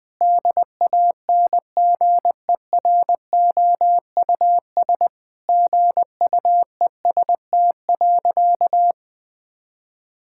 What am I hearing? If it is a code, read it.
Morse code, heard as a keyed tone, about 21 wpm, 700 Hz. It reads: DANGEROUS GUEST.
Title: Sentences from Top 1600 Words 20wpm